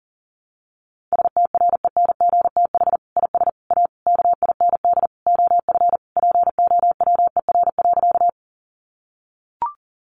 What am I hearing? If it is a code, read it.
Morse code, heard as a keyed tone, about 40 wpm, 700 Hz. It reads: STRENGTH IS A KIND OF POWER.
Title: Sentences from Top 1300 Words 40wpm